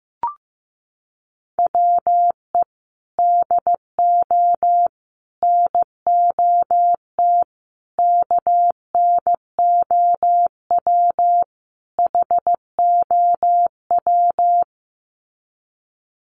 Morse code, heard as a keyed tone, about 15 wpm, 700 Hz. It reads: WE DO NOT KNOW HOW
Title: Sets of 5 Words Encore 15wpm